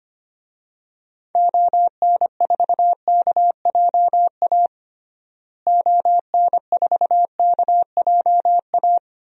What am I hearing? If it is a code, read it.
Morse code, heard as a keyed tone, about 25 wpm, 700 Hz. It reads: ON4KJA ON4KJA
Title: DXCC-100 Call Signs - 6 Characters - Repeated 2x - 25wpm